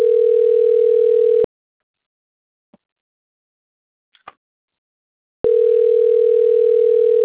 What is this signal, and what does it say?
Ringback tone (North America), 440 + 480 Hz.
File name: out- - - - - .56526